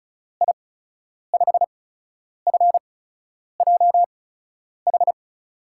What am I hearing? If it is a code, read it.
Morse code, heard as a keyed tone, about 36 wpm, spaced as for 10 wpm, 700 Hz. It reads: I5FJH
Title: USA Call Signs - 6 Characters - 8x Character Spacing - 35wpm